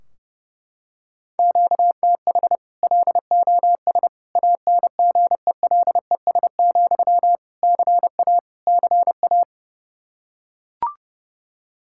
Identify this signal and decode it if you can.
Morse code, heard as a keyed tone, about 30 wpm, 700 Hz. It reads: QTH LOS ANGELES, CA CA
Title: Practice QSO - Exchange 1 - Beginning - 30wpm